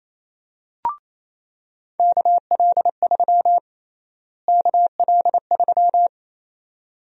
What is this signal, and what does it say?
Morse code, heard as a keyed tone, about 28 wpm, 700 Hz. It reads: KL3 KL3
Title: USA Call Sign Prefixes - Repeated 2x - 28wpm